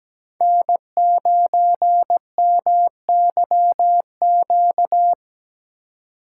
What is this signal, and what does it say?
Morse code, heard as a keyed tone, about 17 wpm, 700 Hz. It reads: N9MYQ
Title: USA Call Signs - 5 Characters - 2x Character Spacing - 17wpm